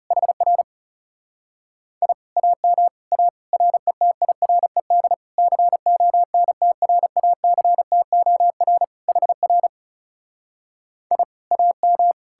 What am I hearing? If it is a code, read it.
Morse code, heard as a keyed tone, about 37 wpm, 700 Hz. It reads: HR I AM A RETIRED CONTRACTOR HR I AM
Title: Speed-Racing Common QSO Phrases - 30wpm